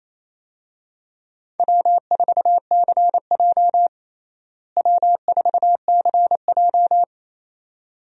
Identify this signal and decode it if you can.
Morse code, heard as a keyed tone, about 28 wpm, 700 Hz. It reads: W4CJ W4CJ
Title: USA Call Signs - 4 Characters - Repeated 2x - 28wpm